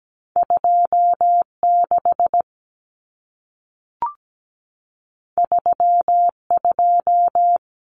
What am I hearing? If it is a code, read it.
Morse code, heard as a keyed tone, about 17 wpm, 700 Hz. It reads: 26 32
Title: Numbers - 2-Digit 17wpm